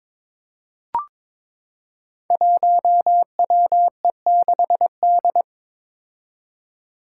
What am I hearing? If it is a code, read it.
Morse code, heard as a keyed tone, about 22 wpm, 700 Hz. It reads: 1WE6D
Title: Five Challenging Letter-Number Combinations 22wpm